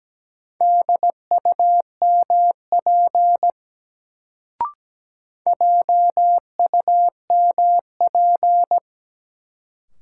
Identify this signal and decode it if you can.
Morse code, heard as a keyed tone, about 17 wpm, 700 Hz. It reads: DUMP JUMP
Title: Sentences from Top 1300 Words - New Word Drill Alt - 17wpm